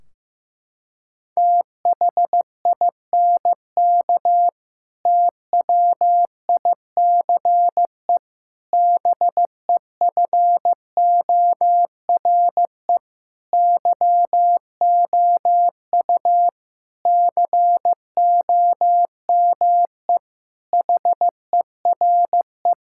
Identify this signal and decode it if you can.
Morse code, heard as a keyed tone, about 16 wpm, 700 Hz. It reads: THINK TWICE BEFORE YOU COME HERE
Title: Sentences from Top 1600 Words 15wpm